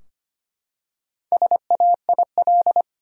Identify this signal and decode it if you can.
Morse code, heard as a keyed tone, about 25 wpm, 700 Hz. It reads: SAIL
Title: Four Letter Words - 25wpm